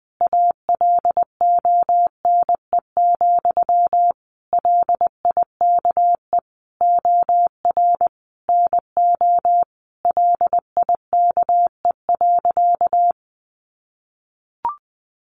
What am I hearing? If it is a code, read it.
Morse code, heard as a keyed tone, about 20 wpm, 700 Hz. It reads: ALONE, LIKE OR NO LIKE.
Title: Sentences from Top 1000 Words 20wpm